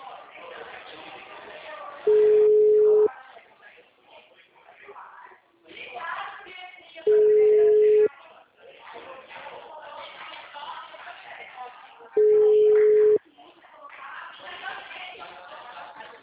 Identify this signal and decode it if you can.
Ringback tone (Europe, 425 Hz), with speech.